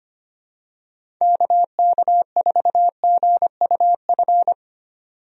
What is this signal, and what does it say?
Morse code, heard as a keyed tone, about 25 wpm, 700 Hz. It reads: KK4GUF
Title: USA Call Signs - 6 Characters - 25wpm